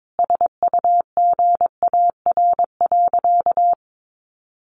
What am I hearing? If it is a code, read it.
Morse code, heard as a keyed tone, about 22 wpm, 700 Hz. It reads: SUGAR.
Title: Sentences from Top 1200 Words 22wpm